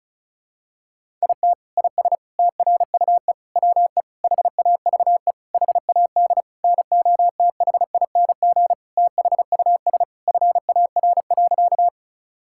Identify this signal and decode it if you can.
Morse code, heard as a keyed tone, about 36 wpm, 700 Hz. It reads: IT IS TRUE WE HAVE HAD NOTHING THUS FAR.
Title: Sentences from Top 1000 Words - Review Part 5 - 35wpm